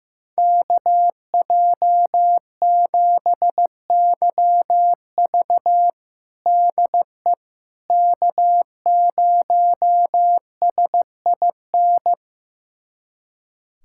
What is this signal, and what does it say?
Morse code, heard as a keyed tone, about 16 wpm, 700 Hz. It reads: KJ7YV DE K0SIN
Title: USA Callsign DE Callsign - 5 Characters - 15wpm